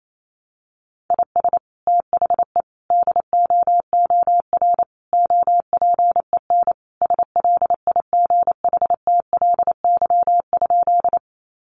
Morse code, heard as a keyed tone, about 28 wpm, 700 Hz. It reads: IS THE DOOR OPEN SLIGHTLY?
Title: Sentences from Top 1300 Words 28wpm